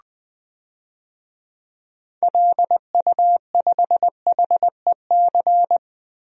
Morse code, heard as a keyed tone, about 20 wpm, 700 Hz. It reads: LU5HEC
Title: DXCC-100 Call Signs - 6 Characters - Repeated 2x - 20wpm